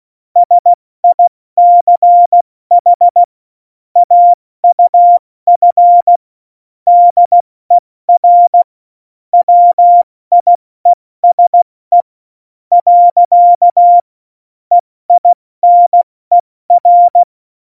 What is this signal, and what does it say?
Morse code, heard as a keyed tone, about 16 wpm, spaced as for 14 wpm, 700 Hz. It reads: SICH AUF DER WIESE . EINER
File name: Morsekurs / Witze / Witz 20 March 2018, 70 F80